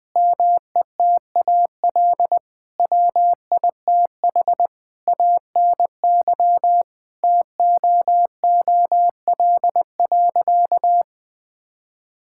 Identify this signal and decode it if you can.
Morse code, heard as a keyed tone, about 20 wpm, 700 Hz. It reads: METAL WITH ANY TOOL.
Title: Sentences from Top 1000 Words - Review Part 6 - 20wpm